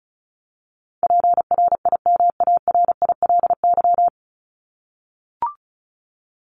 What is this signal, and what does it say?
Morse code, heard as a keyed tone, about 34 wpm, 700 Hz. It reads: PRIMARILY
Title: LY Suffix - Top 100 Words with Base Words 35wpm